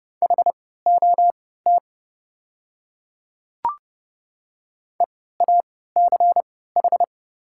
Morse code, heard as a keyed tone, about 31 wpm, spaced as for 17 wpm, 700 Hz. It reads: HOT EACH
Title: 3x Character Spacing - Top 100 Words - 30wpm